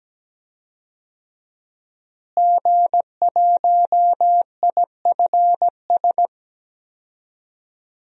Morse code, heard as a keyed tone, about 17 wpm, 700 Hz. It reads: G1IFS
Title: DXCC-100 Call Signs - 5 Characters - Repeated 3x - 17wpm